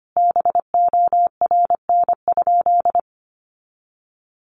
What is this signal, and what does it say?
Morse code, heard as a keyed tone, about 25 wpm, 700 Hz. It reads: BORN?
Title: Sentences from Top 1000 Words 25wpm